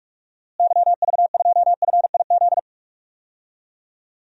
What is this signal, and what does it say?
Morse code, heard as a keyed tone, about 42 wpm, 700 Hz. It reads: YV2FIZ
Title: DXCC-100 Call Signs - 6 Characters - Repeated 2x - 45wpm